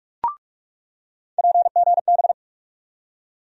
Morse code, heard as a keyed tone, about 43 wpm, 700 Hz. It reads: PGB
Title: USA Call Sign Postfixes - 3 Letters 45wpm